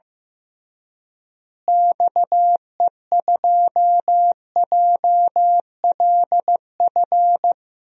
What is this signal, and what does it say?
Morse code, heard as a keyed tone, about 15 wpm, 700 Hz. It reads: XE2JLF
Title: DXCC-100 Call Signs - 6 Characters - Repeated 3x - 15wpm